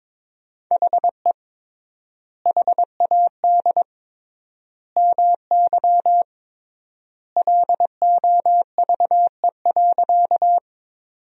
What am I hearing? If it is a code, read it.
Morse code, heard as a keyed tone, about 21 wpm, 700 Hz. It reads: HE HAD MY LOVE.
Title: 3x Word-Spacing - Sentences from Top 400 Words - 22wpm